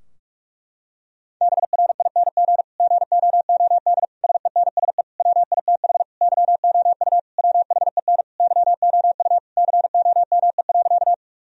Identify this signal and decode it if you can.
Morse code, heard as a keyed tone, about 46 wpm, 700 Hz. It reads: BRING GOOD SENSE WITH YOU WHEN YOU COME.
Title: Sentences from Top 1000 Words - Review Part 4 - 45wpm